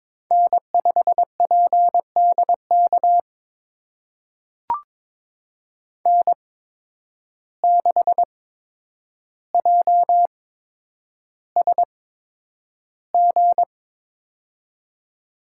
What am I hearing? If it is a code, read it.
Morse code, heard as a keyed tone, about 21 wpm, spaced as for 9 wpm, 700 Hz. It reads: N5PDK N6JSG
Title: USA Call Signs - 8x Character Spacing - 22wpm